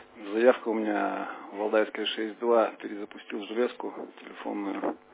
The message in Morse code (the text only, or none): none